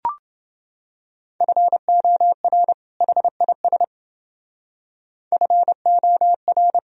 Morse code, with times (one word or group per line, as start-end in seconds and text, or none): 1.4-2.7 FOR
3.0-3.8 HIS
5.3-6.8 FOR